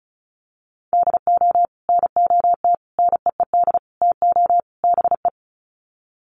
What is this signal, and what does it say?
Morse code, heard as a keyed tone, about 34 wpm, 700 Hz. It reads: DO NOT NEED TO BE
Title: Sets of 5 Words Encore 35wpm